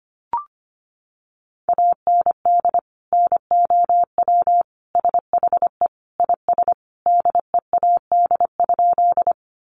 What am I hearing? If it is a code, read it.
Morse code, heard as a keyed tone, about 25 wpm, 700 Hz. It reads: AND NOW SHE IS DEAD?
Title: Sentences from Top 1500 Words 25wpm